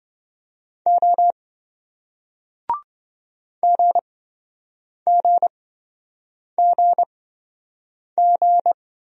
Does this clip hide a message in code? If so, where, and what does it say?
Yes: Morse code from 0.9 to 8.7 s, heard as a keyed tone, about 28 wpm, spaced as for 7 wpm, 700 Hz. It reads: O GGGG